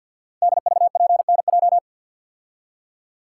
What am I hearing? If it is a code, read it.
Morse code, heard as a keyed tone, about 49 wpm, 700 Hz. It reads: DVPNJ
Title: Code Groups - 5 Letters - 50wpm